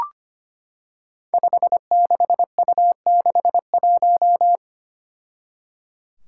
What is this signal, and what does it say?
Morse code, heard as a keyed tone, about 25 wpm, 700 Hz. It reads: 56U61
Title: Five Challenging Letter-Number Combinations 25wpm